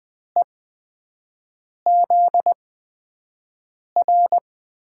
Morse code, heard as a keyed tone, about 20 wpm, spaced as for 5 wpm, 700 Hz. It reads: EZR
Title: USA Call Signs - 5 Characters - 8x Character Spacing - 20wpm